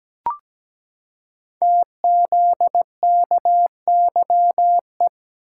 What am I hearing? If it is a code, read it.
Morse code, heard as a keyed tone, about 17 wpm, 700 Hz. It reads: TZKYE
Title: Code Groups - 5 Letters - 17wpm